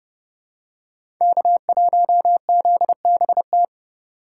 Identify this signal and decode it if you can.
Morse code, heard as a keyed tone, about 29 wpm, 700 Hz. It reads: K1ZBT
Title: USA Call Signs - 5 Characters - 3x Character Spacing - 30wpm